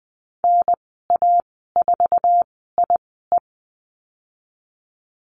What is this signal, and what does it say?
Morse code, heard as a keyed tone, about 20 wpm, spaced as for 15 wpm, 700 Hz. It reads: NA4IE
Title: 076 Lesson 13g - Letter L - Call Signs 20wpm